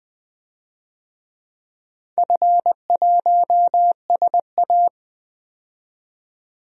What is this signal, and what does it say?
Morse code, heard as a keyed tone, about 20 wpm, 700 Hz. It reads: F1SA
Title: DXCC-100 Call Signs - 4 Characters - 20wpm